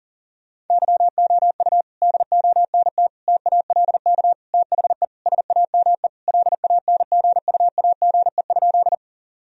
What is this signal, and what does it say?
Morse code, heard as a keyed tone, about 40 wpm, 700 Hz. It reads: YOU DONT TALK THE SAME LANGUAGE?
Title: Sentences from Top 1000 Words - Review Part 3 - 40wpm